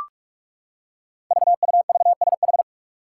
Morse code, heard as a keyed tone, about 44 wpm, 700 Hz. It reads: VUVSH
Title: Five Challenging Letter-Number Combinations 45wpm